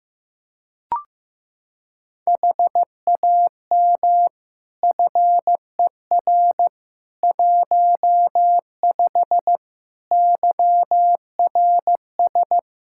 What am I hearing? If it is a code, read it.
Morse code, heard as a keyed tone, about 15 wpm, 700 Hz. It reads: HAM FER 15 YRS
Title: Practice QSO - Exchange 3 - More Info - 15wpm